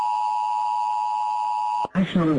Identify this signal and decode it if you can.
Emergency Alert System (SAME) tones (the 853 and 960 Hz attention tone), with speech. It is heard from 0.0 to 1.9 s.